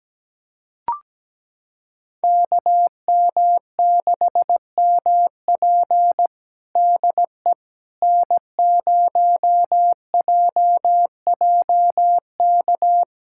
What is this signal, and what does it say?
Morse code, heard as a keyed tone, about 17 wpm, 700 Hz. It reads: KM6MP DE N0JJK
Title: USA Callsign DE Callsign - 5 Characters - 17wpm